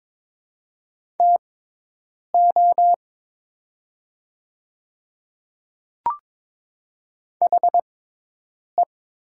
Morse code, heard as a keyed tone, about 22 wpm, spaced as for 8 wpm, 700 Hz. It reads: TO HE
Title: 6x Character Spacing - Top 100 Words - 22wpm